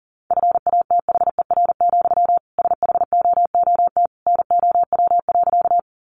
Morse code, heard as a keyed tone, about 40 wpm, 700 Hz. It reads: FATHER, SHOOT NOW.
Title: Sentences from Top 1200 Words 40wpm